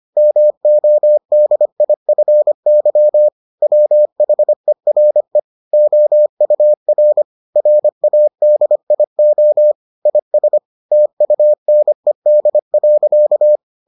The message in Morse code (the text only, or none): MODIFY WHERE OUR RADIO IS TUNED.